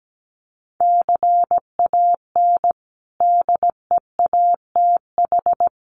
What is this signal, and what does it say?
Morse code, heard as a keyed tone, about 17 wpm, 700 Hz. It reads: CAN DEATH
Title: Sentences from Top 1200 Words 17wpm